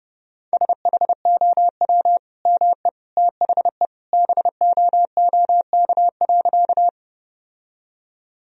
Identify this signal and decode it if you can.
Morse code, heard as a keyed tone, about 30 wpm, 700 Hz. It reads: SHOW ME THE BOOK.
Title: Sentences from Top 1000 Words - Review Part 1 - 30wpm